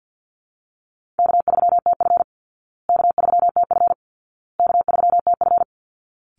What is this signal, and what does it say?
Morse code, heard as a keyed tone, about 50 wpm, 700 Hz. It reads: K3TF K3TF K3TF
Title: Call Signs Repeated 2x - 50wpm